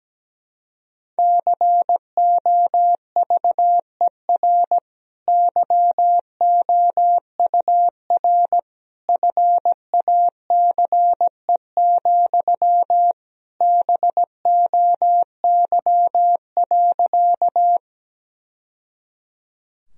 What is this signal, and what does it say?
Morse code, heard as a keyed tone, about 17 wpm, 700 Hz. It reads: COVER YOUR FACE, BOY.
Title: Sentences from Top 1000 Words - Review Part 1 - 17wpm